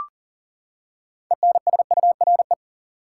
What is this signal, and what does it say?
Morse code, heard as a keyed tone, about 40 wpm, 700 Hz. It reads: ENSURE
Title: Six Letter Words - 40wpm